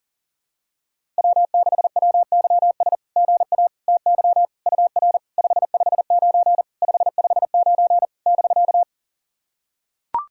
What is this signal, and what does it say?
Morse code, heard as a keyed tone, about 40 wpm, 690 Hz. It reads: W6WYS GA TY UR 559 559 <BK>
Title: SOTA QSO Practice 40wpm